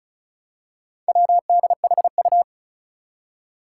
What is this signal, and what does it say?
Morse code, heard as a keyed tone, about 34 wpm, 700 Hz. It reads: WDHU
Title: Four Challenging Letter-Number Combinations 35wpm